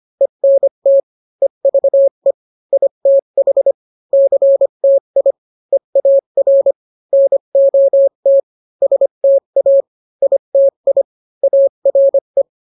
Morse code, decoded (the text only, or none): ENT EVE ITH CTI EAR NOT STA ITI ARE